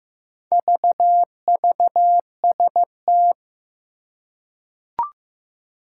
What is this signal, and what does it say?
Morse code, heard as a keyed tone, about 15 wpm, 700 Hz. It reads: VVST
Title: Four Challenging Letter-Number Combinations 15wpm